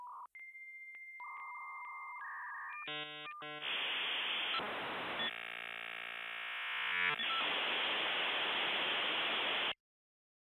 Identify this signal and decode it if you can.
Dial-up modem handshake.